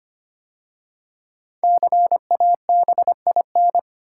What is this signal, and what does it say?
Morse code, heard as a keyed tone, about 25 wpm, 700 Hz. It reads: CABIN
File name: 5 Letter Words - First 3 Letters Given - 25wpm